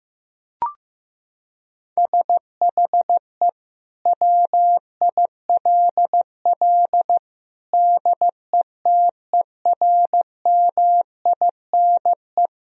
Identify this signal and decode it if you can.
Morse code, heard as a keyed tone, about 15 wpm, 700 Hz. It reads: SHE WILL DETERMINE
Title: Sentences from Top 1000 Words - Review Part 4 - 15wpm